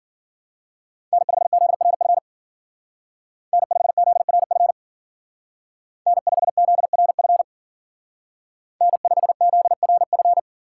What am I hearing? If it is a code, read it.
Morse code, heard as a keyed tone, about 52 wpm, 700 Hz. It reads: N5ZRF N5ZRF N5ZRF N5ZRF